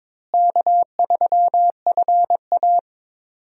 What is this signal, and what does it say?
Morse code, heard as a keyed tone, about 22 wpm, 700 Hz. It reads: K3FA
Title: USA Call Signs - 4 Characters - Repeated 3x - 22wpm